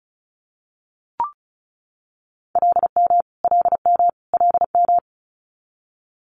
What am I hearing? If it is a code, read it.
Morse code, heard as a keyed tone, about 36 wpm, 700 Hz. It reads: LM LM LM
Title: Multi-letter Phonemes Repeated 3x 35wpm